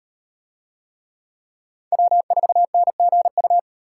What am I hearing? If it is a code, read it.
Morse code, heard as a keyed tone, about 36 wpm, 700 Hz. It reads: W4NGU